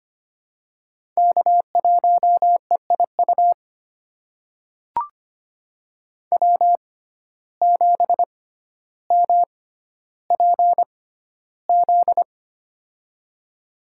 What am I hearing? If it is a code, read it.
Morse code, heard as a keyed tone, about 25 wpm, spaced as for 13 wpm, 700 Hz. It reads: K1EIU W7MPZ
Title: USA Call Signs - 5 Characters - 6x Character Spacing - 25wpm